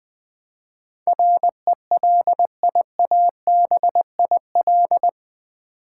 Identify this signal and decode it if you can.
Morse code, heard as a keyed tone, about 20 wpm, 700 Hz. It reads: RELIABIL
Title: Build A Word - 4 Letters At A Time - 11-Letter Words - 20wpm